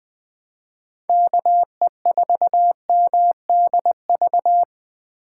Morse code, heard as a keyed tone, about 20 wpm, 700 Hz. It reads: KE4MDV